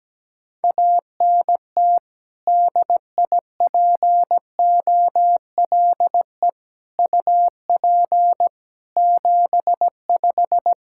Morse code, heard as a keyed tone, about 17 wpm, 700 Hz. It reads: ANT DIPOLE UP 75
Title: K1USN SST QSO Practice 17wpm